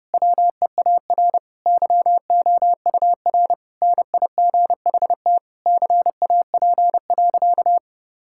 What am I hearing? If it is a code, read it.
Morse code, heard as a keyed tone, about 30 wpm, 700 Hz. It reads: WEAR YOUR NIGHT CAP.
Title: Sentences from Top 1600 Words 30wpm